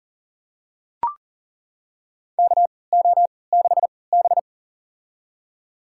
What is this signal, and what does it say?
Morse code, heard as a keyed tone, about 40 wpm, spaced as for 23 wpm, 700 Hz. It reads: KO6B
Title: USA Call Signs - 4 Characters - 3x Character Spacing - 40wpm